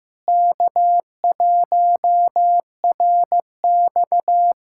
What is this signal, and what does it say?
Morse code, heard as a keyed tone, about 15 wpm, 700 Hz. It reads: K1RX
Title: K1USN SST QSO Practice - 15wpm